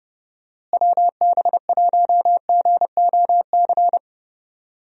Morse code, heard as a keyed tone, about 32 wpm, 700 Hz. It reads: WB1GOC
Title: USA Call Signs - 6 Characters - Repeated 3x - 30wpm